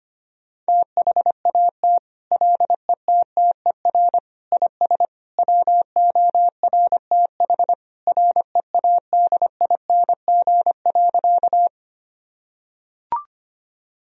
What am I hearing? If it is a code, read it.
Morse code, heard as a keyed tone, about 25 wpm, 700 Hz. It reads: THAT LETTER IS WORTH READING.
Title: Sentences from Top 1000 Words - Review Part 3 - 25wpm